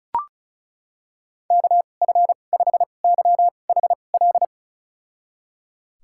Morse code, heard as a keyed tone, about 34 wpm, spaced as for 25 wpm, 700 Hz. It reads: KF5YHL
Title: USA Call Signs - 6 Characters - 2x Character Spacing - 35wpm